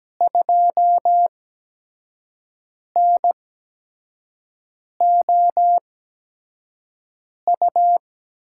Morse code, heard as a keyed tone, about 17 wpm, spaced as for 5 wpm, 700 Hz. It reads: 2NOU
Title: USA Call Signs - 6 Characters - 8x Character Spacing - 17wpm